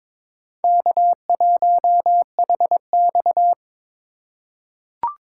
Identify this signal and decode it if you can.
Morse code, heard as a keyed tone, about 22 wpm, 700 Hz. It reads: K1HX
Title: USA Call Signs - 4 Characters - 4x Character Spacing - 22wpm